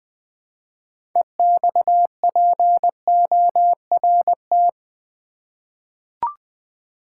Morse code, heard as a keyed tone, about 20 wpm, 700 Hz. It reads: EXPORT
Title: 6 Letter Words - First 3 Letters Given - 20wpm